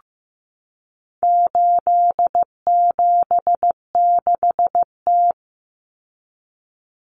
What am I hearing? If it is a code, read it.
Morse code, heard as a keyed tone, about 15 wpm, 700 Hz. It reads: 876T
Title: Common Cut Numbers - 4-Digit 15wpm